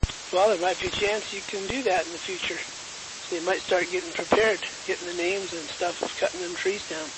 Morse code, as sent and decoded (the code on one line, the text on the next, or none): none